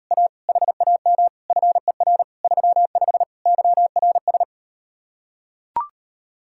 Morse code, heard as a keyed tone, about 38 wpm, 700 Hz. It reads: A HAM FER 35 YRS